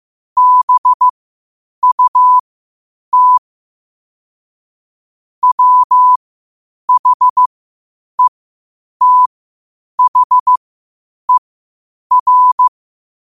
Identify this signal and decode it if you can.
Morse code, heard as a keyed tone, about 15 wpm, spaced as for 8 wpm, 1000 Hz. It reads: BUT WHETHER
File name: qotdfiles / 8wpm